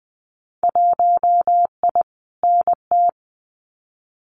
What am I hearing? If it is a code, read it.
Morse code, heard as a keyed tone, about 20 wpm, 700 Hz. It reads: 1I NT
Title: Winter Field Day Exchanges Encore 20wpm